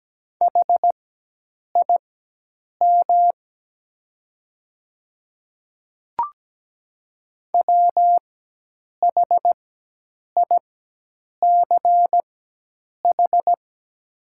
Morse code, heard as a keyed tone, about 17 wpm, spaced as for 8 wpm, 700 Hz. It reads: HIM WHICH
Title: 4x Character Spacing - Top 100 Words - 17wpm